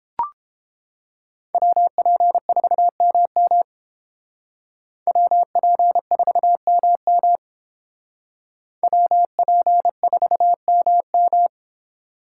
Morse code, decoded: WP4MM WP4MM WP4MM

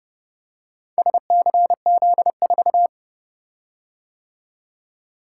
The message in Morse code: SCZ4

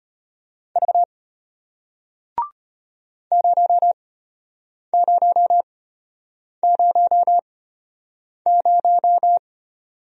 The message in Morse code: V 0000

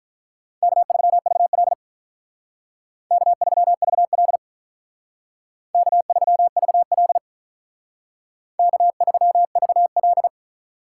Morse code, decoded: K3VL K3VL K3VL K3VL

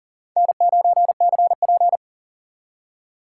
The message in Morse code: N9CP

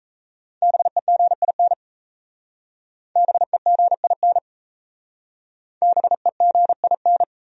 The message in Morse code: BEGIN BEGIN BEGIN

